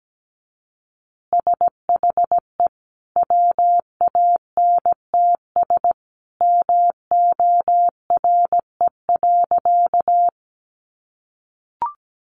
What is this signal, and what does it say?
Morse code, heard as a keyed tone, about 17 wpm, 700 Hz. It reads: SHE WANTS MORE.
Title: Easy Phrases 17wpm